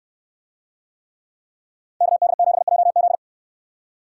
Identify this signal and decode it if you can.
Morse code, heard as a keyed tone, about 64 wpm, 700 Hz. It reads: KD7JZ